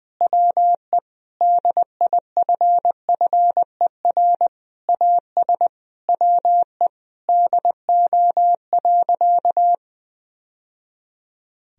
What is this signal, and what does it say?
Morse code, heard as a keyed tone, about 20 wpm, 700 Hz. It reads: WE DIFFER AS WE DO.
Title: Sentences from Top 1000 Words - Review Part 1 - 20wpm